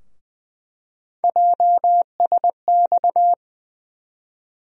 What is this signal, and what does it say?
Morse code, heard as a keyed tone, about 20 wpm, 700 Hz. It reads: JSX
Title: USA Call Sign Postfixes - 3 Letters 20wpm